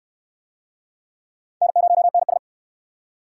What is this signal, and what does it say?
Morse code, heard as a keyed tone, about 68 wpm, 700 Hz. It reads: N0NS